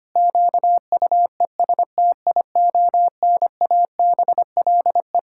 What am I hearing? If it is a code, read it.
Morse code, heard as a keyed tone, about 25 wpm, 700 Hz. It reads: QUESTIONABLE
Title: Pangram Challenge v2 - 25wpm